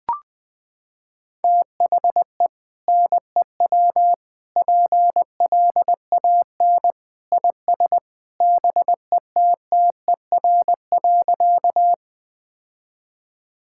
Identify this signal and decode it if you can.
Morse code, heard as a keyed tone, about 20 wpm, 700 Hz. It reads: THE NEW PLAN IS BETTER.
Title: Sentences with 2-Word Phrases - 20wpm